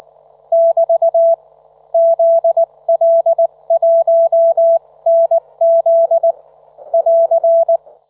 Morse code, heard as a keyed tone, about 19 wpm, spaced as for 17 wpm, 670 Hz. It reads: <BT> ZL1NZ <AR>
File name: Trivia